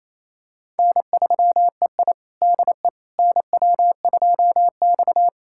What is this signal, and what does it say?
Morse code, heard as a keyed tone, about 28 wpm, 700 Hz. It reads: N3EI DE NW2X
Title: USA Callsign DE Callsign - 4 Characters - 28wpm